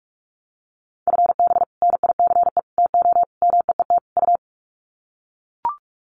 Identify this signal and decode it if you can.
Morse code, heard as a keyed tone, about 44 wpm, 700 Hz. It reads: FB NICE TO MEET U
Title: Common QSO Phrases 45wpm